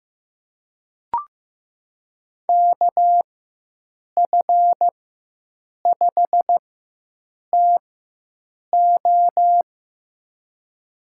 Morse code, heard as a keyed tone, about 15 wpm, spaced as for 7 wpm, 700 Hz. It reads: KF5TO
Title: USA Call Signs - 5 Characters - 4x Character Spacing - 15wpm